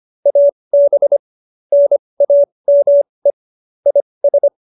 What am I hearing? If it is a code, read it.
Morse code, heard as a keyed tone, about 25 wpm, spaced as for 20 wpm, 570 Hz. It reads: AB NAME IS